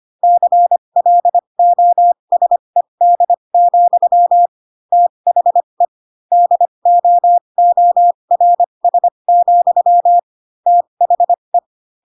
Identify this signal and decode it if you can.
Morse code, heard as a keyed tone, about 25 wpm, spaced as for 22 wpm, 700 Hz. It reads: CLOSED, THE DOORS, THE
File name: Pinocchio / Pinocchio ch6 18 25 ik1hge